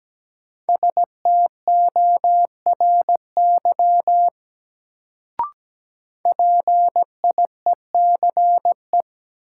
Morse code, heard as a keyed tone, about 17 wpm, 700 Hz. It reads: STORY PIECE
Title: Sets of 3 Words - New Word Drill - 17wpm